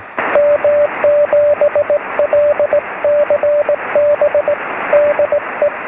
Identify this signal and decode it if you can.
Morse code, heard as a keyed tone, about 19 wpm, 590 Hz, with radio noise. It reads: M7LCB DE
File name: sm7lcb / 12-sm7ecm